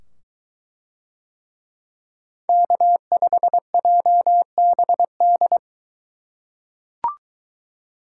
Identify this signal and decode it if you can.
Morse code, heard as a keyed tone, about 22 wpm, 700 Hz. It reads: K5JBD